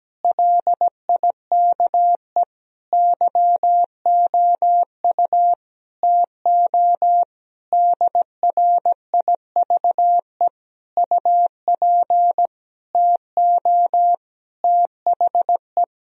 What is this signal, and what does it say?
Morse code, heard as a keyed tone, about 17 wpm, 700 Hz. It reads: LIKE YOU TO DRIVE UP TO THE
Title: Sentences from Top 1000 Words - Review Part 2 - 17wpm